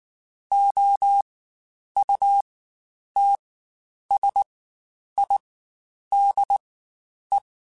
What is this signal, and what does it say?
Morse code, heard as a keyed tone, about 19 wpm, spaced as for 9 wpm, 800 Hz. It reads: OUTSIDE
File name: Spaced Characters / Spaced Characters-mp3s / Weather 3 17 sp @18